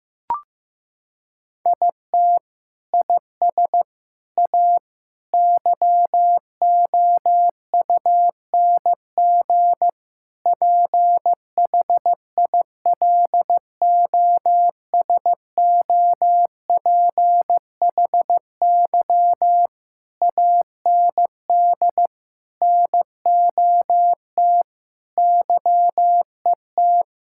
Morse code, heard as a keyed tone, about 15 wpm, 700 Hz. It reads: IT IS A YOUNG PHILOSOPHY AND NOT YET
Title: Sentences from Top 1000 Words - Review Part 6 - 15wpm